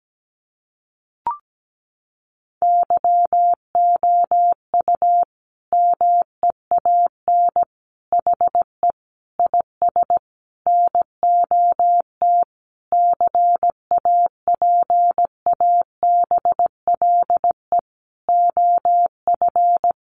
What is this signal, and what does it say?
Morse code, heard as a keyed tone, about 17 wpm, 700 Hz. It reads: YOU MEAN HE IS NOT CAPABLE OF
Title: Sentences from Top 1200 Words 17wpm